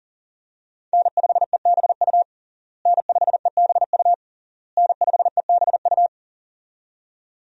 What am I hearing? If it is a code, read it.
Morse code, heard as a keyed tone, about 40 wpm, 700 Hz. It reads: N5EBU N5EBU N5EBU
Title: USA Call Signs - 5 Characters - Repeated 3x - 40wpm